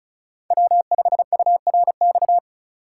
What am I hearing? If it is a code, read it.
Morse code, heard as a keyed tone, about 36 wpm, 700 Hz. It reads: W5URX
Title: USA Call Signs - 5 Characters - 4x Character Spacing - 35wpm